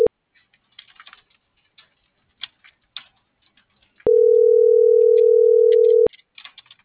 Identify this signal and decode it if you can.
Ringback tone (North America), 440 + 480 Hz.